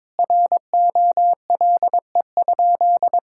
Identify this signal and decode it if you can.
Morse code, heard as a keyed tone, about 22 wpm, 700 Hz. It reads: ROLE?
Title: Sentences from Top 1000 Words - Review Part 6 - 22wpm